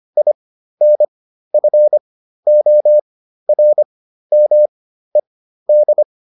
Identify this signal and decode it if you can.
Morse code, heard as a keyed tone, about 25 wpm, spaced as for 13 wpm, 600 Hz. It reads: INFORMED